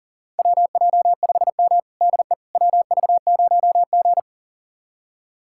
Morse code, heard as a keyed tone, about 40 wpm, 700 Hz. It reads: WJ5M DE WV0G
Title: USA Callsign DE Callsign - 4 Characters - 40wpm